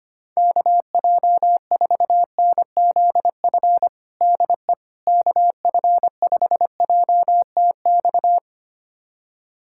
Morse code, heard as a keyed tone, about 25 wpm, 700 Hz. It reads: KJ4NZF DE KF5JTX
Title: USA Callsign DE Callsign - 6 Characters - 25wpm